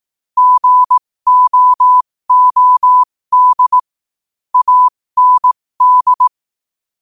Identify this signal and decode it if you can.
Morse code, heard as a keyed tone, about 18 wpm, spaced as for 15 wpm, 1000 Hz. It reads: GOOD AND
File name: qotdfiles / 15wpm